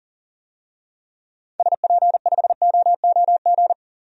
Morse code, heard as a keyed tone, about 37 wpm, 700 Hz. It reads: SP5OOG
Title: DXCC-100 Call Signs - 6 Characters - Repeated 3x - 40wpm